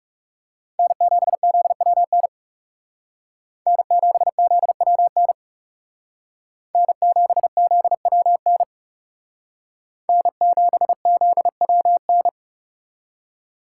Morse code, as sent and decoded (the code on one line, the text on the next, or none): -. --... --.. .-- -. / -. --... --.. .-- -. / -. --... --.. .-- -. / -. --... --.. .-- -.
N7ZWN N7ZWN N7ZWN N7ZWN